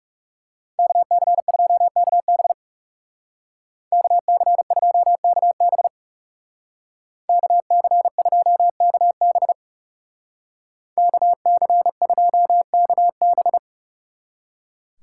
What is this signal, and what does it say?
Morse code, heard as a keyed tone, about 37 wpm, 700 Hz. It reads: KC2KB KC2KB KC2KB KC2KB
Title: Speed Racing - USA Call Signs 30wpm